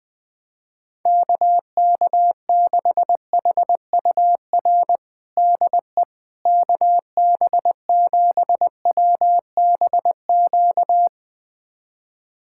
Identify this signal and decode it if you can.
Morse code, heard as a keyed tone, about 21 wpm, 700 Hz. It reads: KK6HUR DE KB7WBQ
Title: USA Callsign DE Callsign - 6 Characters - 20wpm